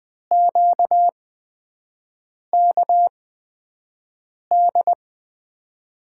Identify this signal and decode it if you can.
Morse code, heard as a keyed tone, about 20 wpm, spaced as for 5 wpm, 700 Hz. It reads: QKD
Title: USA Call Signs - 6 Characters - 8x Character Spacing - 20wpm